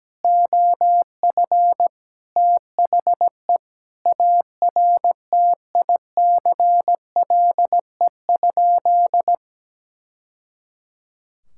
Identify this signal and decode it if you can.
Morse code, heard as a keyed tone, about 17 wpm, 700 Hz. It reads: OF THE ARTICLE?
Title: Sentences from Top 1000 Words - Review Part 5 - 17wpm